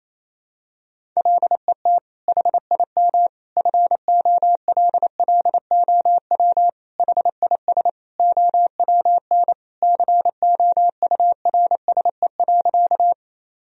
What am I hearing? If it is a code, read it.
Morse code, heard as a keyed tone, about 27 wpm, 700 Hz. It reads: LET HIM FOLLOW HIS OWN COURSE.
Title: Sentences from Top 1000 Words - Review Part 1 - 28wpm